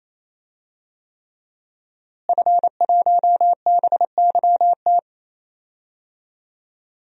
Morse code, heard as a keyed tone, about 27 wpm, 700 Hz. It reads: F1BYT
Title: DXCC-100 Call Signs - 5 Characters - Repeated 3x - 28wpm